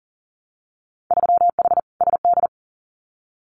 Morse code, heard as a keyed tone, about 39 wpm, 700 Hz. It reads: 3H SD
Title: Winter Field Day Exchanges Encore 40wpm